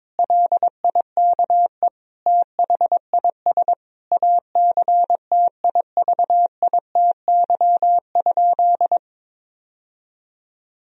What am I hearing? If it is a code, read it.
Morse code, heard as a keyed tone, about 22 wpm, 700 Hz. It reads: LIKE THIS ACTIVITY?
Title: Sentences from Top 1000 Words - Review Part 4 - 22wpm